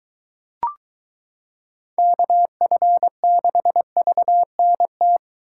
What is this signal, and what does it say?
Morse code, heard as a keyed tone, about 22 wpm, 700 Hz. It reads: KF6VNT